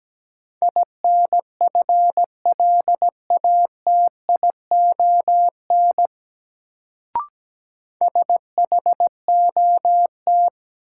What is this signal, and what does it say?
Morse code, heard as a keyed tone, about 17 wpm, 700 Hz. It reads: INFLATION SHOT
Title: Sentences from Top 1100 Words - New Word Drill Alt - 17wpm